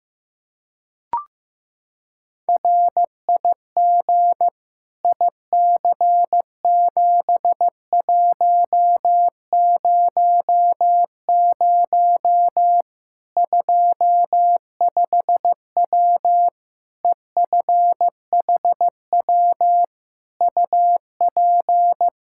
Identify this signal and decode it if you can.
Morse code, heard as a keyed tone, about 15 wpm, 700 Hz. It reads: RIG IC7100 25W EFHW UP
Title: Practice QSO - Exchange 2 - Info - 15wpm